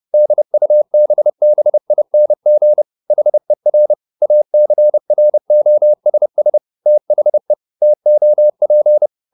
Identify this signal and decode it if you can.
Morse code, heard as a keyed tone, about 30 wpm, 600 Hz. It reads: DUBBING HER ACROSS THE TOP
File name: the old man and the sea / the old man and the sea 1 30wpm